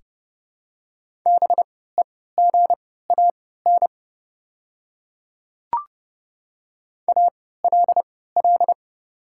Morse code, heard as a keyed tone, about 30 wpm, spaced as for 17 wpm, 700 Hz. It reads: BEGAN ALL
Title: 3x Character Spacing - Top 300 Words - 30wpm